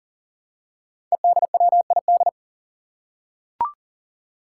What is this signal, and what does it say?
Morse code, heard as a keyed tone, about 37 wpm, 700 Hz. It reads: EDWID
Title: Five Challenging Letter-Number Combinations 40wpm